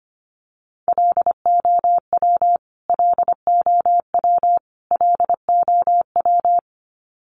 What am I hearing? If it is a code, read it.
Morse code, heard as a keyed tone, about 25 wpm, 700 Hz. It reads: LOW LOW LOW
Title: Top 300 Words Repeated 25wpm